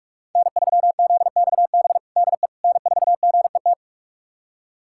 Morse code, heard as a keyed tone, about 46 wpm, 700 Hz. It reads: N3ZXB DE N4GET
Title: USA Callsign DE Callsign - 5 Characters - 45wpm